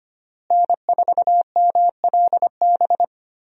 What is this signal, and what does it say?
Morse code, heard as a keyed tone, about 25 wpm, 700 Hz. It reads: N4MLB
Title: USA Call Signs - 5 Characters - Repeated 3x - 25wpm